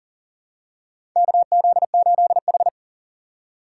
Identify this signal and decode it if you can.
Morse code, heard as a keyed tone, about 40 wpm, 700 Hz. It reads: KZ8H
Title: USA Call Signs - 4 Characters - 2x Character Spacing - 40wpm